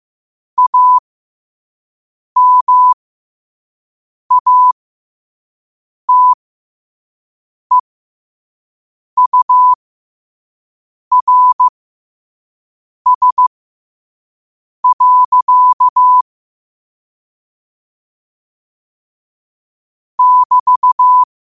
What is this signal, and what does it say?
Morse code, heard as a keyed tone, about 15 wpm, spaced as for 5 wpm, 1000 Hz. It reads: AMATEURS. <BT>
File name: qotdfiles / 5wpm